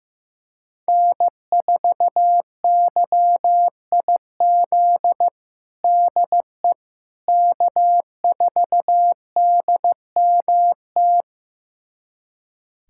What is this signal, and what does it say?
Morse code, heard as a keyed tone, about 16 wpm, 700 Hz. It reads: N4YIZ DE K4DMT
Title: USA Callsign DE Callsign - 5 Characters - 15wpm